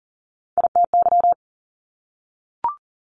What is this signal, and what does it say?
Morse code, heard as a keyed tone, about 42 wpm, 700 Hz. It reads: ITY
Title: Top Three Letter Combinations in Words 40wpm